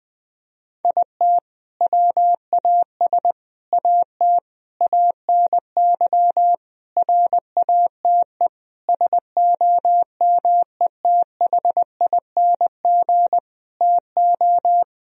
Morse code, heard as a keyed tone, about 21 wpm, 700 Hz. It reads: IT WAS AT ANY RATE SOMETHING TO
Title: Sentences from Top 1000 Words - Review Part 4 - 20wpm